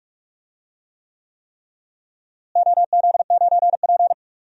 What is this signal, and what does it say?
Morse code, heard as a keyed tone, about 46 wpm, 700 Hz. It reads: OZ9P
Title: DXCC-100 Call Signs - 4 Characters - Repeated 3x - 45wpm